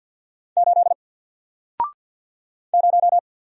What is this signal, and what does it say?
Morse code, heard as a keyed tone, about 50 wpm, 700 Hz. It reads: 8 0
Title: Single Character (Letter:Number:Punctuation:Prosign) 50wpm